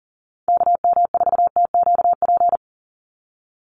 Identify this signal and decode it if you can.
Morse code, heard as a keyed tone, about 40 wpm, 700 Hz. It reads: KM4TQP
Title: USA Call Signs 40wpm